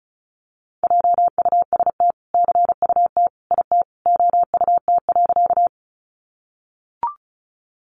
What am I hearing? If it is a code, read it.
Morse code, heard as a keyed tone, about 34 wpm, 700 Hz. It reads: JUST CUT IT OUT.
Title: Sentences from Top 300 Words 35wpm